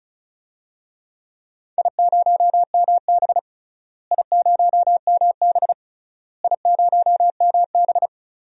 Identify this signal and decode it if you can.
Morse code, heard as a keyed tone, about 36 wpm, 700 Hz. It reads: I0MB I0MB I0MB
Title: DXCC-100 Call Signs - 4 Characters - Repeated 3x - 35wpm